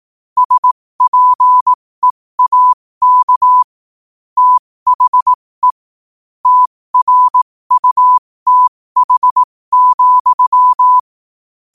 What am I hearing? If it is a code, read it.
Morse code, heard as a keyed tone, about 18 wpm, spaced as for 15 wpm, 1000 Hz. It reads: SPEAK THE TRUTH,
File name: qotdfiles / 15wpm